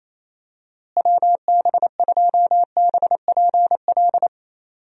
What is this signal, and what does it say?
Morse code, heard as a keyed tone, about 28 wpm, 700 Hz. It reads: WB2BPL
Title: USA Call Signs - 6 Characters - Repeated 3x - 28wpm